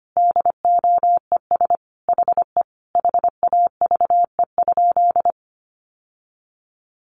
Morse code, heard as a keyed tone, about 25 wpm, 700 Hz. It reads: DOES HE HAVE?
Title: Sentences from Top 1000 Words 25wpm